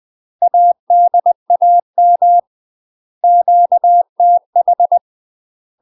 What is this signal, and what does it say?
Morse code, heard as a keyed tone, about 20 wpm, 700 Hz. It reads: ADAM QTH
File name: UK MASTER QSOs TRAINING / 20WPM / UK MASTER QSOs 0062